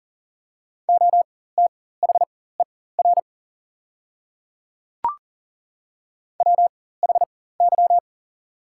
Morse code, heard as a keyed tone, about 40 wpm, spaced as for 19 wpm, 700 Hz. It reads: OTHER WHY
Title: 4x Character Spacing - Top 200 Words - 40wpm